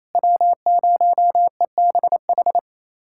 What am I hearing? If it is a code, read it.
Morse code, heard as a keyed tone, about 28 wpm, 700 Hz. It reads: W0EBH
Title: USA Callsign DE Callsign - 5 Characters - 28wpm